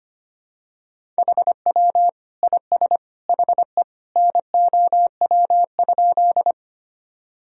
Morse code, heard as a keyed tone, about 25 wpm, 700 Hz. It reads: HW IS HE NOW?
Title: Sentences from Top 100 Words and Abbreviations 25wpm